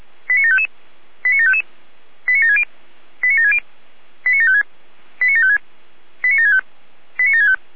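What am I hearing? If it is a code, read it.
Five-tone selective call (ZVEI), heard as quick runs of five tones, tones of about 80 ms. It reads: 89755 89755 89760 89760 89756 89756 89765 89765
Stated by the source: ZVEI five-tone call received by SDR.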